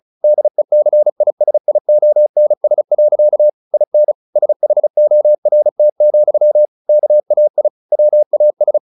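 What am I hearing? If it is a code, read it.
Morse code, heard as a keyed tone, about 34 wpm, 600 Hz. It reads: DECISIONS. IN SHORT, KAI WAS
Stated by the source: Morse practice at 35 wpm from CW med Gnister, sending Ken Follett's novel Never in English.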